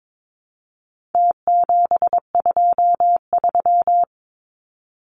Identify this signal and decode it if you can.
Morse code, heard as a keyed tone, about 22 wpm, 700 Hz. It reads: T723
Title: Numbers - Contest Style - 4-Digit 22wpm